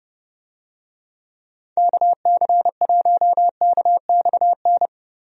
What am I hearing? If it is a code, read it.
Morse code, heard as a keyed tone, about 30 wpm, 700 Hz. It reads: KC1KXN
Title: USA Call Signs - 6 Characters - Repeated 2x - 30wpm